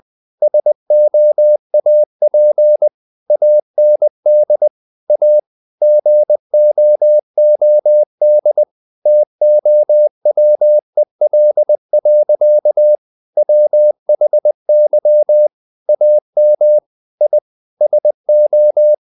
Morse code, heard as a keyed tone, about 20 wpm, 600 Hz. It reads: SOAP AND A GOOD TOWEL. WHY AM I SO
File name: the old man and the sea / the old man and the sea part 1 20wpm